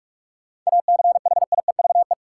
Morse code, heard as a keyed tone, about 46 wpm, 700 Hz. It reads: ACHIEVE
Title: Sentences from Top 1000 Words - Review Part 5 - New Word Drill - 45wpm